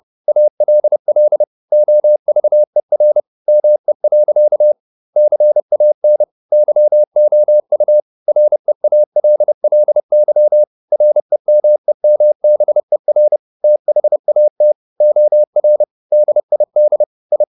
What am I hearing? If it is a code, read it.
Morse code, heard as a keyed tone, about 30 wpm, 600 Hz. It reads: ALL OVER ME. CAN YOU REALLY REMEMBER THAT OR DID I
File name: the old man and the sea / the old man and the sea 1 30wpm